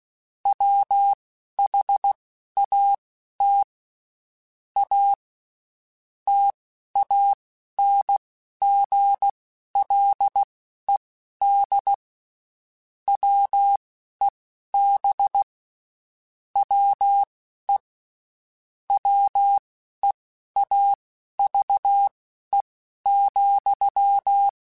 Morse code, heard as a keyed tone, about 16 wpm, spaced as for 11 wpm, 800 Hz. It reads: WHAT A TANGLED WEB WE WEAVE,
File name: Quotes Spaced / Quotes Spaced-mp3s / Deceive sp @15